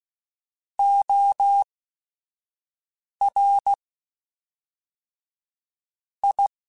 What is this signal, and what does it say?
Morse code, heard as a keyed tone, about 16 wpm, spaced as for 4 wpm, 800 Hz. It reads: ORI